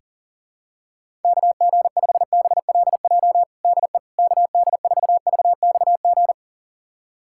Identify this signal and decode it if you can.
Morse code, heard as a keyed tone, about 40 wpm, 700 Hz. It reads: KG5BLJ DE KD4VXG
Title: USA Callsign DE Callsign 40wpm